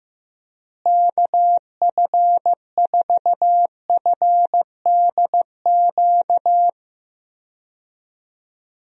Morse code, heard as a keyed tone, about 15 wpm, 700 Hz. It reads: KF4FDQ
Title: USA Call Signs - 6 Characters - Repeated 3x - 15wpm